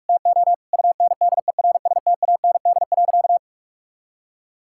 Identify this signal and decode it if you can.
Morse code, heard as a keyed tone, about 41 wpm, 700 Hz. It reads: TO UNDERSTAND.
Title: Sentences from Top 1000 Words - Review Part 6 - 45wpm